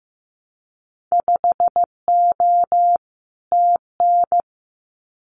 Morse code, heard as a keyed tone, about 15 wpm, 700 Hz. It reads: 5O TN
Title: Winter Field Day Exchanges Encore 15wpm